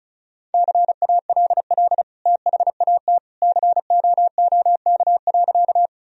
Morse code, heard as a keyed tone, about 34 wpm, 700 Hz. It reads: CALL THAT COOK.
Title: Sentences from Top 1000 Words - Review Part 4 - 35wpm